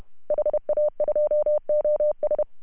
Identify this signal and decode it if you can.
Morse code, heard as a keyed tone, about 31 wpm, 600 Hz. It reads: HA2OS
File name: CW / HA2OS RHOB 2017 CW